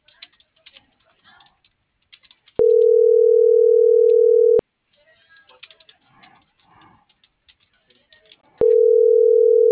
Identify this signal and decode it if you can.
Ringback tone (North America), 440 + 480 Hz.